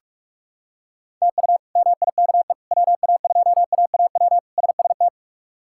Morse code, heard as a keyed tone, about 42 wpm, 700 Hz. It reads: TU MIKE WA2AAW SST
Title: K1USN SST QSO Practice - 45wpm